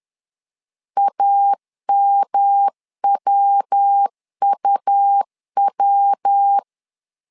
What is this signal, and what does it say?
Morse code, heard as a keyed tone, about 10 wpm, 800 Hz. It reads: AMWUW
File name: LEZIONI / 3 MWRO / 3 MW 10wpm 1959